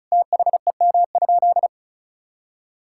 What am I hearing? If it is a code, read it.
Morse code, heard as a keyed tone, about 36 wpm, 700 Hz. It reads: THEM?
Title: Sentences from Top 1000 Words - Review Part 6 - 35wpm